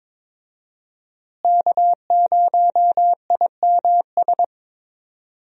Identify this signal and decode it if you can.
Morse code, heard as a keyed tone, about 21 wpm, 700 Hz. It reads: K0IMS
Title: USA Call Signs - 5 Characters - Repeated 2x - 22wpm